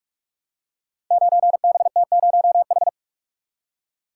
Morse code, heard as a keyed tone, about 44 wpm, 700 Hz. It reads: 9BT0H
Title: Code Groups - 5 Letter:Numbers - 45wpm